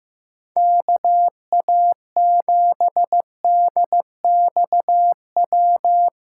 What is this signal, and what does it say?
Morse code, heard as a keyed tone, about 15 wpm, 700 Hz. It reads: KA7DXW
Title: USA Callsign DE Callsign 15wpm